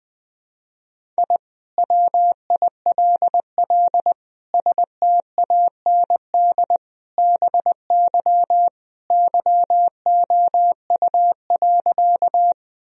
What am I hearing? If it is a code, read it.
Morse code, heard as a keyed tone, about 20 wpm, 700 Hz. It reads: I WILL STAND BY YOU.
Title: Sentences from Top 1000 Words - Review Part 1 - 20wpm